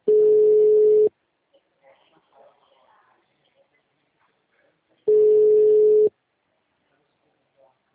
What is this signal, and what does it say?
Ringback tone (Europe, 425 Hz).